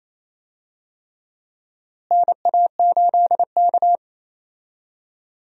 Morse code, heard as a keyed tone, about 28 wpm, 700 Hz. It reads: NA8K
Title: USA Call Signs - 4 Characters - Repeated 3x - 28wpm